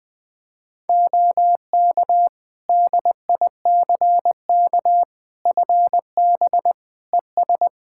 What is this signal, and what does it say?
Morse code, heard as a keyed tone, about 20 wpm, 700 Hz. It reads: OK DICK FB ES
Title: Practice QSO - Exchange 2 - Info - 20wpm